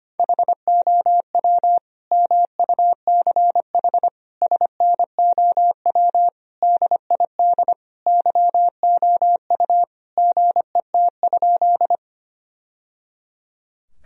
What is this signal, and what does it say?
Morse code, heard as a keyed tone, about 25 wpm, 700 Hz. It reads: HOW MUCH SNOW DID YOU GET?
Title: Sentences from Top 1000 Words - Review Part 3 - 25wpm